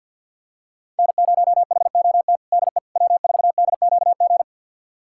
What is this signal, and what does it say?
Morse code, heard as a keyed tone, about 52 wpm, 700 Hz. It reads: N0HOT DE W4DQG
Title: USA Callsign DE Callsign - 5 Characters - 50wpm